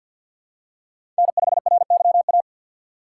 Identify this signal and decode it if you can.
Morse code, heard as a keyed tone, about 52 wpm, 700 Hz. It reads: N5RYA
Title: USA Call Signs - 5 Characters - 3x Character Spacing - 50wpm